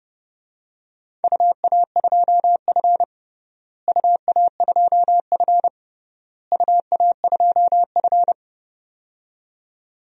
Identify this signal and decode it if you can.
Morse code, heard as a keyed tone, about 30 wpm, 700 Hz. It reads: UA2F UA2F UA2F
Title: DXCC-100 Call Signs - 4 Characters - Repeated 3x - 30wpm